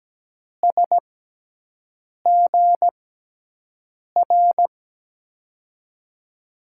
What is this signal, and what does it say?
Morse code, heard as a keyed tone, about 17 wpm, spaced as for 6 wpm, 700 Hz. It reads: SGR
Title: USA Call Signs - 6 Characters - 6x Character Spacing - 17wpm